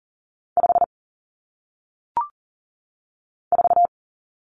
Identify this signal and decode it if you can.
Morse code, heard as a keyed tone, about 40 wpm, 700 Hz. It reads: 5 4
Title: Common Cut Numbers - 1-Digit 40wpm